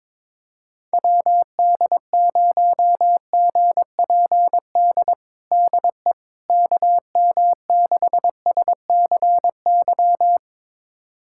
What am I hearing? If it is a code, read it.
Morse code, heard as a keyed tone, about 22 wpm, 700 Hz. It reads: WD0GPD DE KM6SCY
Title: USA Callsign DE Callsign 22wpm